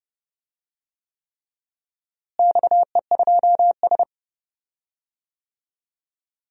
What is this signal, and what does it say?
Morse code, heard as a keyed tone, about 30 wpm, 700 Hz. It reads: XE2S
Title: DXCC-100 Call Signs - 4 Characters - Repeated 3x - 30wpm